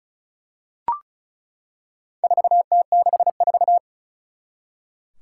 Morse code, heard as a keyed tone, about 36 wpm, 700 Hz. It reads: 4T64
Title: Four Challenging Letter-Number Combinations 35wpm